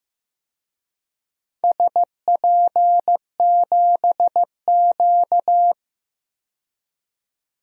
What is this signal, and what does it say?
Morse code, heard as a keyed tone, about 15 wpm, 700 Hz. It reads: SP7Q
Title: DXCC-100 Call Signs - 4 Characters - 15wpm